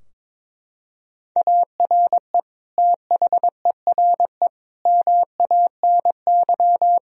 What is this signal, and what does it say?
Morse code, heard as a keyed tone, about 21 wpm, 700 Hz. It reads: ARE THERE MANY
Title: Sentences from Top 1000 Words - Review Part 4 - 22wpm